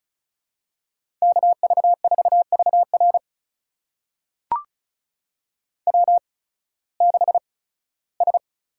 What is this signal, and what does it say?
Morse code, heard as a keyed tone, about 36 wpm, spaced as for 19 wpm, 700 Hz. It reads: KV4VR W6S
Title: USA Call Signs - 8x Character Spacing - 35wpm